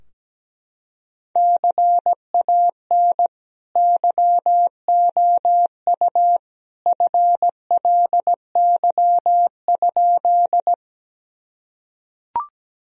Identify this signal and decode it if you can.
Morse code, heard as a keyed tone, about 17 wpm, 700 Hz. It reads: CAN YOU FLY?
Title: Sentences from Top 1000 Words - Review Part 2 - 17wpm